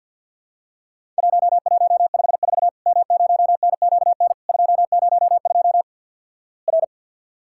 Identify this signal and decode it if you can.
Morse code, heard as a keyed tone, about 48 wpm, 700 Hz. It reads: 1154 M0NQN 202 R
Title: Worked All Europe DX Contest - QTC Exchange - 50wpm